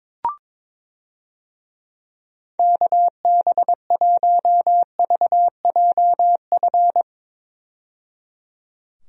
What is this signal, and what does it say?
Morse code, heard as a keyed tone, about 21 wpm, 700 Hz. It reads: KB1VJF
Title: USA Call Signs - 6 Characters - 22wpm